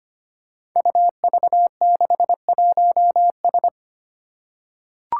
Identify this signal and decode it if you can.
Morse code, heard as a keyed tone, about 25 wpm, 700 Hz. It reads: UV61S
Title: Five Challenging Letter-Number Combinations 25wpm